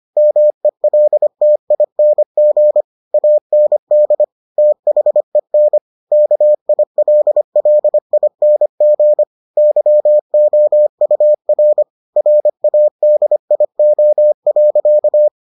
Morse code, heard as a keyed tone, about 25 wpm, 600 Hz. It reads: MELTING AND THEN KILLING YOUR RADIO.